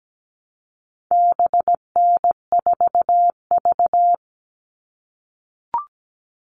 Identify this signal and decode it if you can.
Morse code, heard as a keyed tone, about 17 wpm, 700 Hz. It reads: BN4V
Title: All Cut Numbers - 4-Digit 17wpm